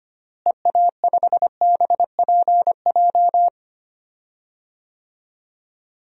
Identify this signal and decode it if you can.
Morse code, heard as a keyed tone, about 25 wpm, 700 Hz. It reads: EA5BPJ
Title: DXCC-100 Call Signs - 6 Characters - Repeated 3x - 25wpm